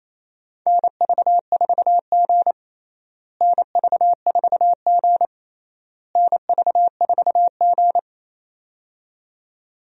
Morse code, heard as a keyed tone, about 27 wpm, 700 Hz. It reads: NV4G NV4G NV4G
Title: USA Call Signs - 4 Characters - Repeated 3x - 28wpm